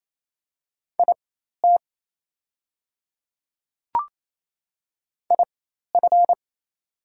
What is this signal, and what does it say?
Morse code, heard as a keyed tone, about 28 wpm, spaced as for 13 wpm, 700 Hz. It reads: IT IF
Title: 4x Character Spacing - Top 100 Words - 28wpm